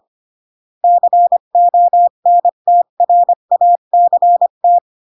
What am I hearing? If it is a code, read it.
Morse code, heard as a keyed tone, about 25 wpm, 700 Hz. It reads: CONTRACT
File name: CW / jodorowsky / 82 commandments 25-20